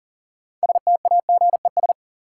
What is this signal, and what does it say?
Morse code, heard as a keyed tone, about 40 wpm, 700 Hz. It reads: STAGES
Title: Sets of 5 Words Encore - New Word Drill - 40wpm